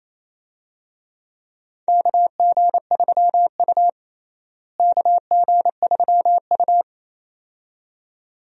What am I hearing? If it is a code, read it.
Morse code, heard as a keyed tone, about 27 wpm, 700 Hz. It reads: KG3U KG3U
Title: USA Call Signs - 4 Characters - Repeated 2x - 28wpm